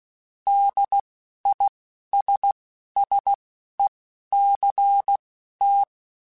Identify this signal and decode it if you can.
Morse code, heard as a keyed tone, about 16 wpm, spaced as for 11 wpm, 800 Hz. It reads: DISSECT
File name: General / General-mp3s / ISS @10